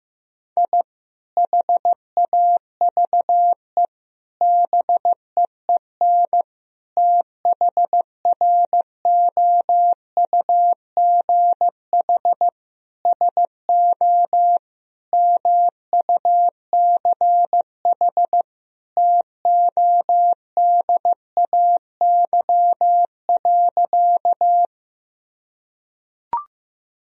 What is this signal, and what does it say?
Morse code, heard as a keyed tone, about 15 wpm, 700 Hz. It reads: I HAVE BEEN THROUGH SO MUCH TODAY.
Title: Sentences from Top 1000 Words - Review Part 4 - 15wpm